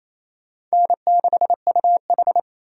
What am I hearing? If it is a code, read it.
Morse code, heard as a keyed tone, about 34 wpm, 700 Hz. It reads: N6UH
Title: Speed Racing - USA 1x2 Call Signs 28wpm